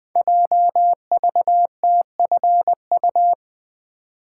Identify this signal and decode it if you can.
Morse code, heard as a keyed tone, about 20 wpm, 700 Hz. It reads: JVTFU
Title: Code Groups - 5 Letter:Numbers - 20wpm